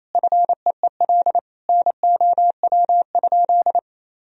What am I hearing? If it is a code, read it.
Morse code, heard as a keyed tone, about 28 wpm, 700 Hz. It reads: FEEL NOW?
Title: Sentences from Top 400 Words and Abbreviations 28wpm